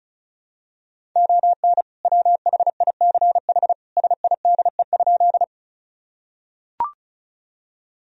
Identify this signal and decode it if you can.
Morse code, heard as a keyed tone, about 36 wpm, 700 Hz. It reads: ON WHICH SIDE?
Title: Sentences from Top 1000 Words - Review Part 1 - 35wpm